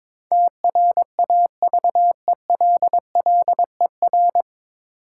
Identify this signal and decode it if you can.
Morse code, heard as a keyed tone, about 22 wpm, 700 Hz. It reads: TRAVELLER
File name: 9 Letter Words - First 3 Letters Given - 22wpm